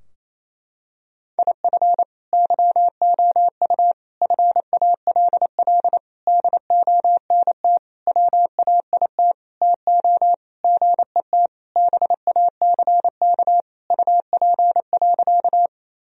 Morse code, heard as a keyed tone, about 28 wpm, 700 Hz. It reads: IF YOU FALL DONT WAIT TO GET BACK UP.
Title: Sentences from Top 1000 Words - Review Part 2 - 28wpm